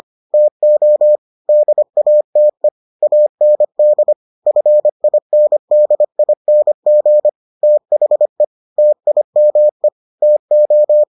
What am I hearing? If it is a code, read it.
Morse code, heard as a keyed tone, about 25 wpm, 600 Hz. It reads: TO DATE AND FINDING THE TIME TO